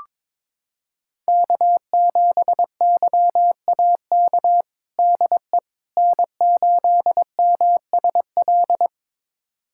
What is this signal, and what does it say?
Morse code, heard as a keyed tone, about 22 wpm, 700 Hz. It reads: K7YAK DE N8MSL
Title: USA Callsign DE Callsign - 5 Characters - 22wpm